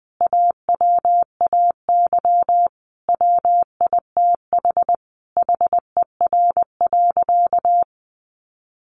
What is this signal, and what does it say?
Morse code, heard as a keyed tone, about 20 wpm, 700 Hz. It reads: AWAY WITH HER.
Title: Sentences from Top 1500 Words 20wpm